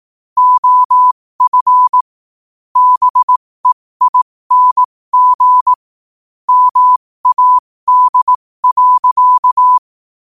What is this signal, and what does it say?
Morse code, heard as a keyed tone, about 18 wpm, spaced as for 15 wpm, 1000 Hz. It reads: OF BEING MAD.
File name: qotdfiles / 15wpm